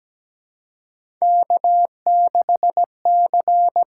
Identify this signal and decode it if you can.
Morse code, heard as a keyed tone, about 17 wpm, 700 Hz. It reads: K 6 C
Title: USA Call Signs - 4 Characters - 6x Character Spacing - 17wpm